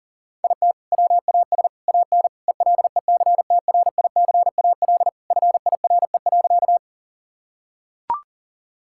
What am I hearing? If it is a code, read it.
Morse code, heard as a keyed tone, about 40 wpm, 690 Hz. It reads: IT WAS AN ELECTRICAL FIRE.
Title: Sentences from Top 1000 Words - Review Part 6 - 40wpm